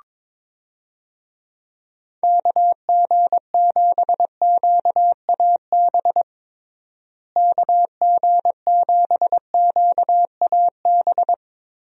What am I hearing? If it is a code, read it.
Morse code, heard as a keyed tone, about 22 wpm, 700 Hz. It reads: KG7QAB KG7QAB
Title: USA Call Signs - 6 Characters - Repeated 3x - 22wpm